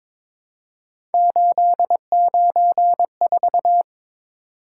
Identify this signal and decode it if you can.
Morse code, heard as a keyed tone, about 22 wpm, 700 Hz. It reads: 894
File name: practice / 22wpm / Three Letter-Number Combinations - 22wpm